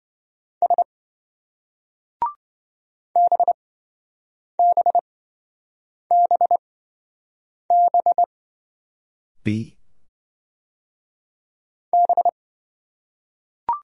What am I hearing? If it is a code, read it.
Morse code, heard as a keyed tone, about 26 wpm, spaced as for 7 wpm, 700 Hz. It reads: S BBBB B